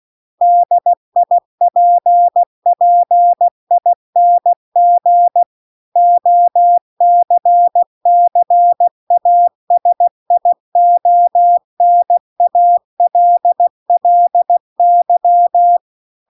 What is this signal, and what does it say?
Morse code, heard as a keyed tone, about 16 wpm, 700 Hz. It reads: DIPPING OCCASIONALLY